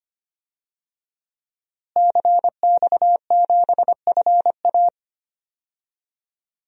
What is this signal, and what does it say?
Morse code, heard as a keyed tone, about 25 wpm, 700 Hz. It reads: CX7FA
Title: DXCC-100 Call Signs - 5 Characters - Repeated 3x - 25wpm